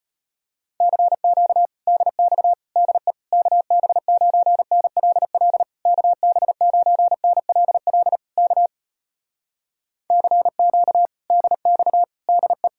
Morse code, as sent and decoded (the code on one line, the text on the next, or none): -.-. --.- / -.. -..- / -.. / -.- -... ----. -. .-.. .-.. / -.- -... ----. -. .-.. .-.. / -.- / -.-. --.- / -.. -..- / -.. .
CQ DX DE KB9NLL KB9NLL K CQ DX DE